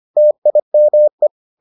Morse code, heard as a keyed tone, about 25 wpm, 600 Hz. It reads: TIME